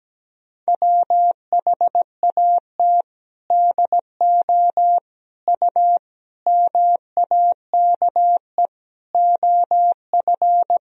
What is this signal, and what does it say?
Morse code, heard as a keyed tone, about 17 wpm, 700 Hz. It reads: WHAT DO U MAKE OF
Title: Sentences from Top 100 Words and Abbreviations 17wpm